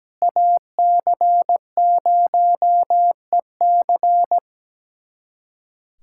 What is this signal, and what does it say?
Morse code, heard as a keyed tone, about 17 wpm, 700 Hz. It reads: AC0EC
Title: USA Callsign DE Callsign - 5 Characters - 17wpm